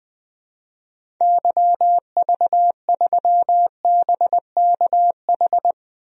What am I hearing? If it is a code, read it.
Morse code, heard as a keyed tone, about 20 wpm, 700 Hz. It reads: YV3BKH
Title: DXCC-100 Call Signs - 6 Characters - 20wpm